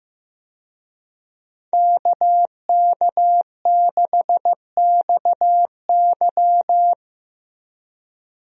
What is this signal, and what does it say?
Morse code, heard as a keyed tone, about 15 wpm, 700 Hz. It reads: KK6XY
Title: USA Call Signs - 5 Characters - Repeated 2x - 15wpm